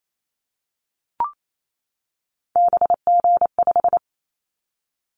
Morse code, heard as a keyed tone, about 28 wpm, 700 Hz. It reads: BG5
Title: Callsign Prefixes for Top 100 DXCC - 28wpm